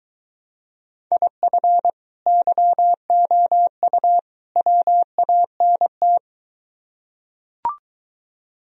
Morse code, heard as a keyed tone, about 23 wpm, 700 Hz. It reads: IF YOU WANT